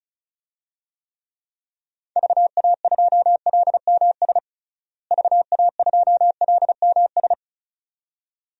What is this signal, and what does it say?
Morse code, heard as a keyed tone, about 36 wpm, 700 Hz. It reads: VA2LMS VA2LMS
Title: DXCC-100 Call Signs - 6 Characters - Repeated 2x - 35wpm